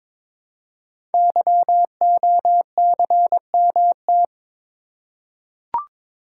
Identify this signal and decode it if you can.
Morse code, heard as a keyed tone, about 22 wpm, 700 Hz. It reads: YOCMT
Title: Code Groups - 5 Letters - 22wpm